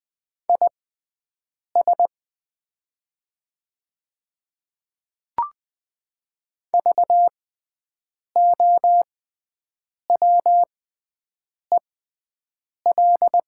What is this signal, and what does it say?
Morse code, heard as a keyed tone, about 20 wpm, spaced as for 7 wpm, 700 Hz. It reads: IS VOWEL
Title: 6x Character Spacing - Top 500 Words - 20wpm